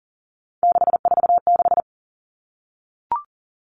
Morse code, heard as a keyed tone, about 40 wpm, 700 Hz. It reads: 646
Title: Numbers - 3-Digit 40wpm